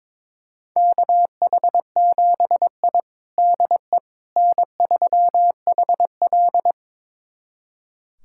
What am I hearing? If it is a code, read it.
Morse code, heard as a keyed tone, about 22 wpm, 700 Hz. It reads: KH7I DE N3HL
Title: USA Callsign DE Callsign - 4 Characters - 22wpm